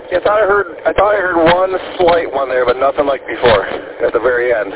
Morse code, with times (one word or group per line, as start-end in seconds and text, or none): none